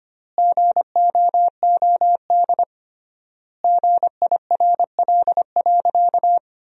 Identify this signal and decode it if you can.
Morse code, heard as a keyed tone, about 25 wpm, 700 Hz. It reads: GOOD GIRL.
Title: 3x Word-Spacing - Sentences from Top 400 Words - 25wpm